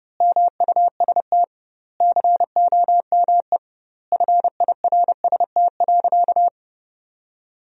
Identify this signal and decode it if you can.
Morse code, heard as a keyed tone, about 30 wpm, 700 Hz. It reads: MUST COME FIRST.
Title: 2x Word-Spacing - Sentences from Top 300 Words - 30wpm